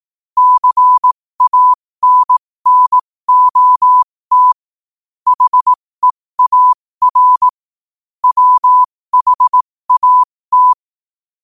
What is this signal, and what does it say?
Morse code, heard as a keyed tone, about 18 wpm, spaced as for 15 wpm, 1000 Hz. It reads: CANNOT HEAR WHAT
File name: qotdfiles / 15wpm